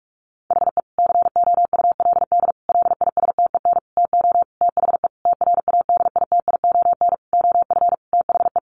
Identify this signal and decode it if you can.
Morse code, heard as a keyed tone, about 44 wpm, 700 Hz. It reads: HE COULD LISTEN TO THE TRADITION OF THE
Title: Sentences from Top 1500 Words 45wpm